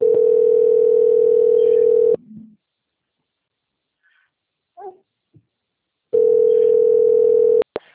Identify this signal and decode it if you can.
Ringback tone (North America), 440 + 480 Hz.